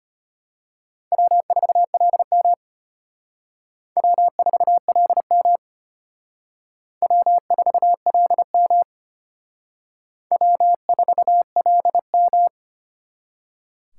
Morse code, heard as a keyed tone, about 32 wpm, 700 Hz. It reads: W4LM W4LM W4LM W4LM